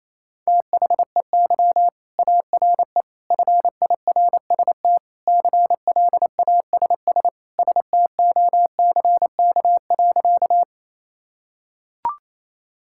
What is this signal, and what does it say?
Morse code, heard as a keyed tone, about 27 wpm, 700 Hz. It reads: THEY ARE FIRST CLASS STOCK.
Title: Sentences from Top 1000 Words - Review Part 5 - 28wpm